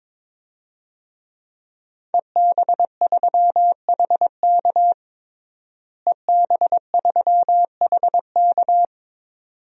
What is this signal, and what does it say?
Morse code, heard as a keyed tone, about 22 wpm, 700 Hz. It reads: EB3HK EB3HK
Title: DXCC-100 Call Signs - 5 Characters - Repeated 2x - 22wpm